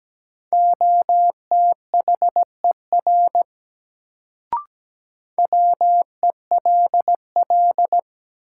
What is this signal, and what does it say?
Morse code, heard as a keyed tone, about 17 wpm, 700 Hz. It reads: OTHER WELL
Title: Sets of 2 Words - New Word Drill - 17wpm